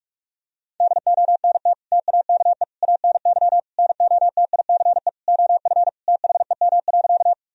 Morse code, heard as a keyed tone, about 41 wpm, 700 Hz. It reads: DONT TAKE ANY NOTICE OF THEM.
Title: Sentences from Top 1000 Words - Review Part 2 - 45wpm